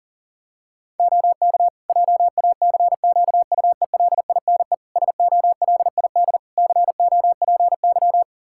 Morse code, heard as a keyed tone, about 40 wpm, 700 Hz. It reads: OK JACQUELINE SOLID COPY
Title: K1USN SST QSO Practice 40wpm